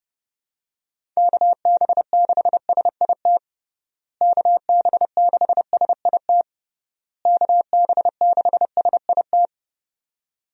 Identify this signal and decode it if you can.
Morse code, heard as a keyed tone, about 32 wpm, 700 Hz. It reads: KB6SIT KB6SIT KB6SIT
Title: USA Call Signs - 6 Characters - Repeated 3x - 30wpm